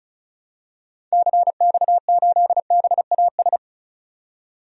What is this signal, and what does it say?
Morse code, heard as a keyed tone, about 34 wpm, 700 Hz. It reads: CX8BAS
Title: DXCC-100 Call Signs - 6 Characters - 35wpm